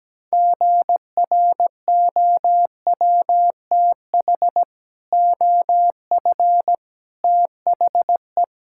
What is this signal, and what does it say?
Morse code, heard as a keyed tone, about 17 wpm, 700 Hz. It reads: GROWTH OF THE
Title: Sentences from Top 1000 Words - Review Part 5 - 17wpm